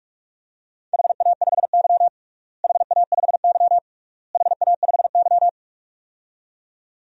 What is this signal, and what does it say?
Morse code, heard as a keyed tone, about 46 wpm, 700 Hz. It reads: HA5Y HA5Y HA5Y
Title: DXCC-100 Call Signs - 4 Characters - Repeated 3x - 45wpm